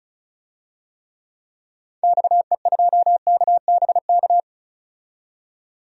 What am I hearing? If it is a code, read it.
Morse code, heard as a keyed tone, about 34 wpm, 700 Hz. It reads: XE2KBK
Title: DXCC-100 Call Signs - 6 Characters - 35wpm